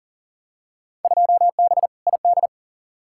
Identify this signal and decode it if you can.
Morse code, heard as a keyed tone, about 40 wpm, 700 Hz. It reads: 2B ID
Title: Field Day Exchanges Encore - 40wpm